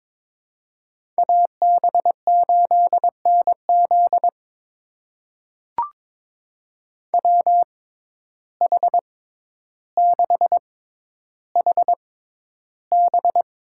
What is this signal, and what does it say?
Morse code, heard as a keyed tone, about 22 wpm, spaced as for 11 wpm, 700 Hz. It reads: AB8NZ WH6HB
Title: USA Call Signs - 5 Characters - 6x Character Spacing - 22wpm